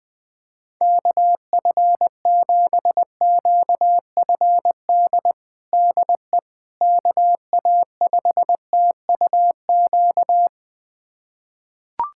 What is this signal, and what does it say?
Morse code, heard as a keyed tone, about 20 wpm, 700 Hz. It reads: KF7QFD DE KA5TUQ
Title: USA Callsign DE Callsign - 6 Characters - 20wpm